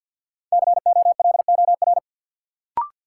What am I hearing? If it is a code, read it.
Morse code, heard as a keyed tone, about 51 wpm, 700 Hz. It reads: COLOR
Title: Sentences from Top 1000 Words - Review Part 2 - New Word Drill - 50wpm